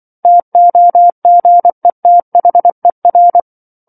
Morse code, heard as a keyed tone, about 24 wpm, 700 Hz. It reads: TOGETHER
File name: 07-Chapter-07 24 WPM